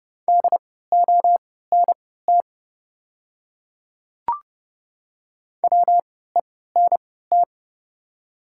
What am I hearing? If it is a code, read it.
Morse code, heard as a keyed tone, about 30 wpm, spaced as for 17 wpm, 690 Hz. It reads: DONT WENT
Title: 3x Character Spacing - Top 300 Words - 30wpm